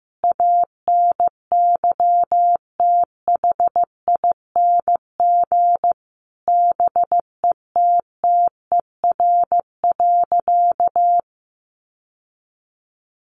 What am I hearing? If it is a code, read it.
Morse code, heard as a keyed tone, about 15 wpm, 700 Hz. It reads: ANYTHING BETTER.
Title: Sentences from Top 1100 Words 15wpm